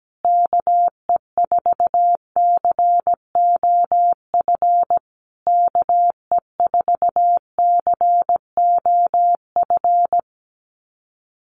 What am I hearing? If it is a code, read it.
Morse code, heard as a keyed tone, about 17 wpm, 700 Hz. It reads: KE4COF KE4COF
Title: USA Call Signs Repeated 3x 17wpm